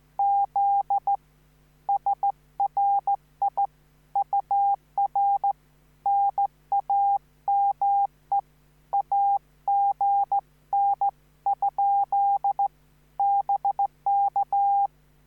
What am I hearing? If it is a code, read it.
Morse code, heard as a keyed tone, about 14 wpm, 800 Hz. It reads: Z SRI UR NAME AGN ? BK
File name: QSO05